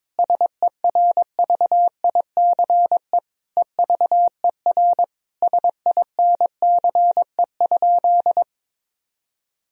Morse code, heard as a keyed tone, about 22 wpm, 700 Hz. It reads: SERVICE EVER SINCE?
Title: Sentences from Top 1000 Words - Review Part 4 - 22wpm